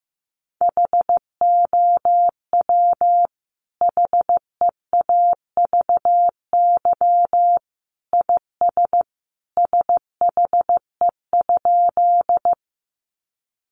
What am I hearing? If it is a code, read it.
Morse code, heard as a keyed tone, about 15 wpm, 700 Hz. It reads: HOW HEAVY IS SHE?
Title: Sentences from Top 1100 Words 15wpm